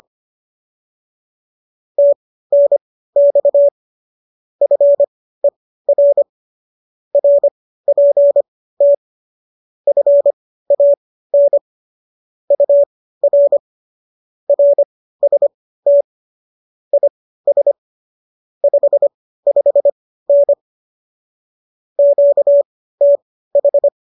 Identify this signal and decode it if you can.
Morse code, heard as a keyed tone, about 25 wpm, spaced as for 15 wpm, 570 Hz. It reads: TNX FER RPT FAN UR RST IS 55N QTH